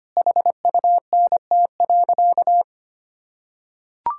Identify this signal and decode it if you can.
Morse code, heard as a keyed tone, about 25 wpm, 700 Hz. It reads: HUNT.
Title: Sentences from Top 1600 Words 25wpm